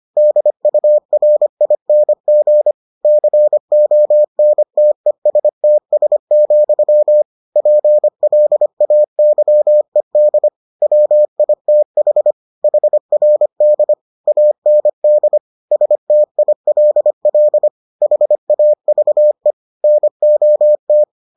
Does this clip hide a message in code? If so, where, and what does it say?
Yes: Morse code from 0.2 to 21.0 s, heard as a keyed tone, about 25 wpm, 600 Hz. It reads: DURING CONTESTS, PLAYED WITH HRD AND STILL HAVE NOT